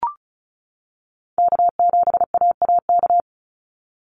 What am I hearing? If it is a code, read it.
Morse code, heard as a keyed tone, about 36 wpm, 700 Hz. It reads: K7AAK
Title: USA Call Signs 35wpm